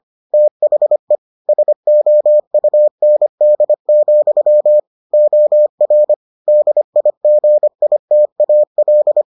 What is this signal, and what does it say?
Morse code, heard as a keyed tone, about 25 wpm, 600 Hz. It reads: THE SOUND, OR DIGITAL